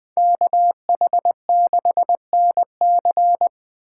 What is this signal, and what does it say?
Morse code, heard as a keyed tone, about 20 wpm, 700 Hz. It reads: KH6NC
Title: Speed Racing - USA 2x2 Call Signs 20wpm